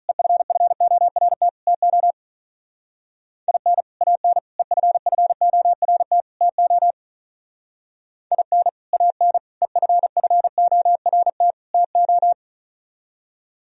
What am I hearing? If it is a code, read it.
Morse code, heard as a keyed tone, about 43 wpm, 700 Hz. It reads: EFFORT TO IN AN EFFORT TO IN AN EFFORT TO
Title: Speed-Racing - Sets of 4 Words 35wpm